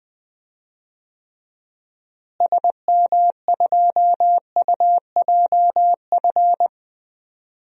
Morse code, heard as a keyed tone, about 21 wpm, 700 Hz. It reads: SM2UJF
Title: DXCC-100 Call Signs - 6 Characters - Repeated 2x - 20wpm